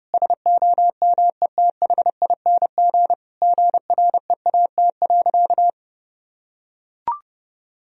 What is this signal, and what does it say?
Morse code, heard as a keyed tone, about 32 wpm, 700 Hz. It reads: SOMETHING GREAT.
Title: Sentences from Top 1000 Words - Review Part 2 - 30wpm